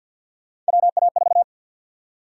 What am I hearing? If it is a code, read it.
Morse code, heard as a keyed tone, about 50 wpm, 700 Hz. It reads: WA4
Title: USA Call Sign Prefixes - 50wpm